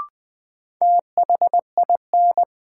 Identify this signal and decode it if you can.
Morse code, heard as a keyed tone, about 20 wpm, 700 Hz. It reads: THIN
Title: Sentences from Top 1200 Words - New Word Drill - 20wpm